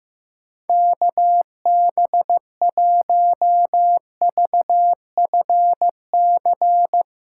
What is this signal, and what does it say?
Morse code, heard as a keyed tone, about 16 wpm, 700 Hz. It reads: KB1VFC
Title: Speed Racing - USA 2x3 Call Signs 15wpm